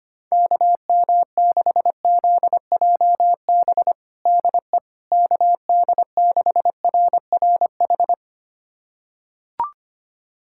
Morse code, heard as a keyed tone, about 25 wpm, 700 Hz. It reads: KM6ZJB DE KD6RRH
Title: USA Callsign DE Callsign 25wpm